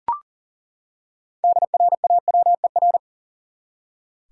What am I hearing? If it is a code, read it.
Morse code, heard as a keyed tone, about 40 wpm, 700 Hz. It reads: DRAWER
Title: Six Letter Words - 40wpm